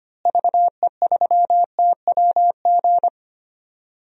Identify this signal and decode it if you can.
Morse code, heard as a keyed tone, about 25 wpm, 700 Hz. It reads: VE3TWG
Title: DXCC-100 Call Signs - 6 Characters - Repeated 3x - 25wpm